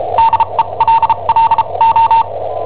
Morse code, heard as a keyed tone, about 34 wpm, spaced as for 27 wpm, 910 Hz. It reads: DELLO
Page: Cassetta esercizi preparata da I7OHP per IW5ELP